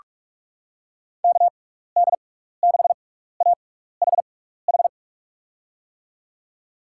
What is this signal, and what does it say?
Morse code, heard as a keyed tone, about 46 wpm, spaced as for 16 wpm, 700 Hz. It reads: KD6AHH
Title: USA Call Signs - 6x Character Spacing - 45wpm